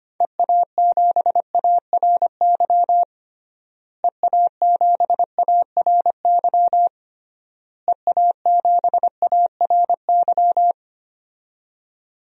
Morse code, heard as a keyed tone, about 25 wpm, 700 Hz. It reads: EA7ARY EA7ARY EA7ARY
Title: DXCC-100 Call Signs - 6 Characters - Repeated 3x - 25wpm